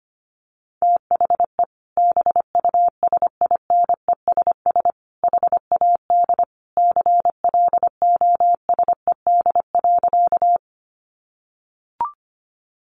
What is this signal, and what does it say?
Morse code, heard as a keyed tone, about 25 wpm, 700 Hz. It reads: THE BUSINESS HAD CLOSED.
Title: Sentences from Top 1400 Words 25wpm